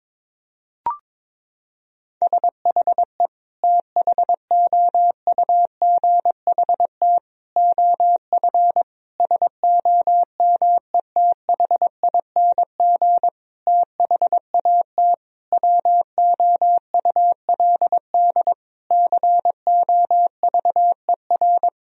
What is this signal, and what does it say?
Morse code, heard as a keyed tone, about 22 wpm, 700 Hz. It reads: SHE THOUGHT OF SOMETHING THAT WOULD COVER
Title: Sentences from Top 1000 Words - Review Part 1 - 22wpm